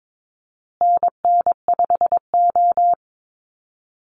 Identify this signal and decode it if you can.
Morse code, heard as a keyed tone, about 22 wpm, 700 Hz. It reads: NN5O
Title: Call Signs Repeated 2x - 22wpm